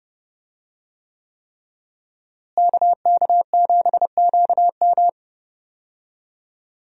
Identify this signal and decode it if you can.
Morse code, heard as a keyed tone, about 30 wpm, 700 Hz. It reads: KK7QM
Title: USA Call Signs - 5 Characters - Repeated 3x - 30wpm